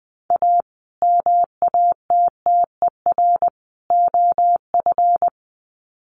Morse code, heard as a keyed tone, about 20 wpm, 700 Hz. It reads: A MATTER OF
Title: Sets of 3 Words 20wpm